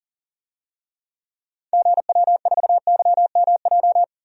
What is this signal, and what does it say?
Morse code, heard as a keyed tone, about 40 wpm, 700 Hz. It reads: GW4YMJ
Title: DXCC-100 Call Signs - 6 Characters - 40wpm